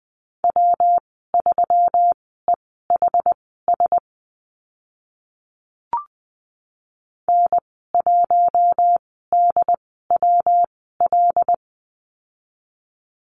Morse code, heard as a keyed tone, about 20 wpm, spaced as for 14 wpm, 700 Hz. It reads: W3EHS N1DWL
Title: 140 Lesson 23g - Letter f - Call Signs 20wpm